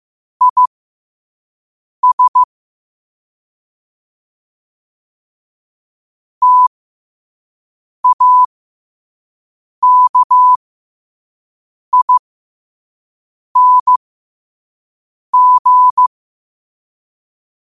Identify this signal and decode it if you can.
Morse code, heard as a keyed tone, about 15 wpm, spaced as for 5 wpm, 1000 Hz. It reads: IS TAKING
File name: qotdfiles / 5wpm